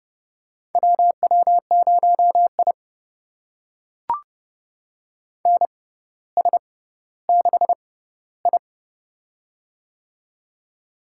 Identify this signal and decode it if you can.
Morse code, heard as a keyed tone, about 31 wpm, spaced as for 16 wpm, 700 Hz. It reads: WW0I NS6I
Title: USA Call Signs - 6x Character Spacing - 30wpm